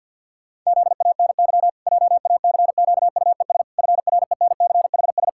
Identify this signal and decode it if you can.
Morse code, heard as a keyed tone, about 50 wpm, 700 Hz. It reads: ZANY JACQUES FRENCHS
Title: Pangram Challenge v2 - 50wpm